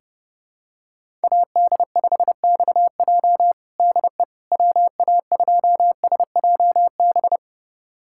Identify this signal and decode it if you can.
Morse code, heard as a keyed tone, about 30 wpm, 700 Hz. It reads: AD5XJ DE WA2SJB
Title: USA Callsign DE Callsign 30wpm